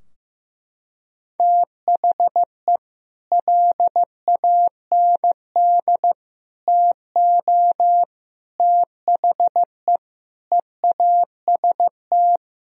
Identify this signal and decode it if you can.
Morse code, heard as a keyed tone, about 16 wpm, 700 Hz. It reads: THE LAND TO THE EAST
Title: Sentences from Top 1000 Words - Review Part 4 - 15wpm